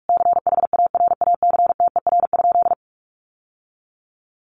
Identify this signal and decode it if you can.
Morse code, heard as a keyed tone, about 44 wpm, 700 Hz. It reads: CHARACTER?
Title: Sentences from Top 1200 Words 45wpm